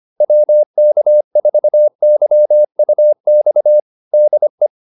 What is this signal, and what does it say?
Morse code, heard as a keyed tone, about 25 wpm, 600 Hz. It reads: WK4YUX DE
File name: QSO305 25